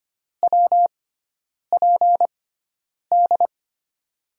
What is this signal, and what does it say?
Morse code, heard as a keyed tone, about 25 wpm, spaced as for 9 wpm, 700 Hz. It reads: WPD
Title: USA Call Signs - 6 Characters - 6x Character Spacing - 25wpm